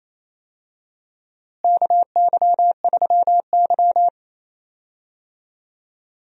Morse code, heard as a keyed tone, about 28 wpm, 700 Hz. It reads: KY3Y
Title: USA Call Signs - 4 Characters - Repeated 2x - 28wpm